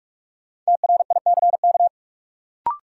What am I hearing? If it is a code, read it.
Morse code, heard as a keyed tone, about 42 wpm, 700 Hz. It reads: TRICK
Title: Sentences from Top 1500 Words - New Word Drill - 45wpm